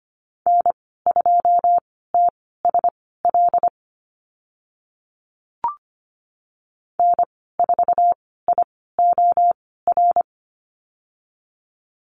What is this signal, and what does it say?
Morse code, heard as a keyed tone, about 25 wpm, spaced as for 16 wpm, 700 Hz. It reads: N2TSL N4IOR
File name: Lesson 15g - Number 5 - Call Signs 25wpm